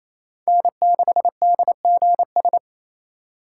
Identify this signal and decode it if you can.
Morse code, heard as a keyed tone, about 28 wpm, 700 Hz. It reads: N6DGS
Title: USA Call Signs - 5 Characters - 2x Character Spacing - 28wpm